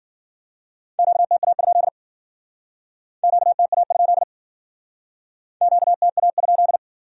Morse code, heard as a keyed tone, about 57 wpm, 700 Hz. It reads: QTA? QTA? QTA?